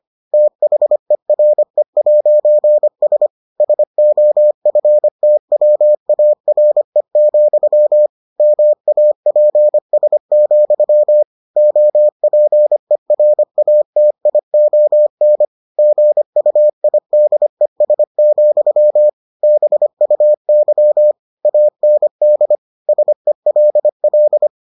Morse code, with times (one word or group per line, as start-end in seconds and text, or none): 0.3-3.3 THERE'S
3.6-8.1 SOFTWARE,
8.4-11.2 MAPS,
11.6-15.5 OPERATION
15.8-19.1 GUIDES,
19.4-21.1 BUY
21.5-22.6 AND
22.9-24.5 SELL